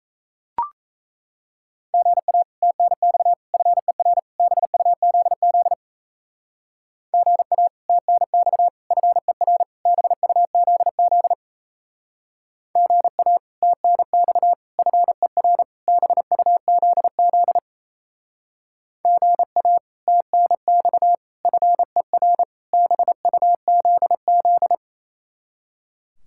Morse code, heard as a keyed tone, about 33 wpm, 700 Hz. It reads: GA TNX FER BUZZ GA TNX FER BUZZ GA TNX FER BUZZ GA TNX FER BUZZ
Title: Speed-Racing Common QSO Phrases Encore - 28wpm